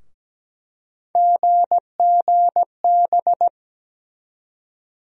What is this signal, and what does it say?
Morse code, heard as a keyed tone, about 17 wpm, 700 Hz. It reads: GGB
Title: USA Call Sign Postfixes - 3 Letters 17wpm